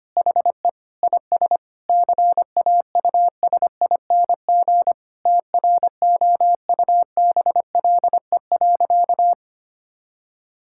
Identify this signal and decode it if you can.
Morse code, heard as a keyed tone, about 25 wpm, 700 Hz. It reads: HE IS CAUSING TROUBLE.
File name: practice / 25wpm / Sentences with 9-Word Phrases - 25wpm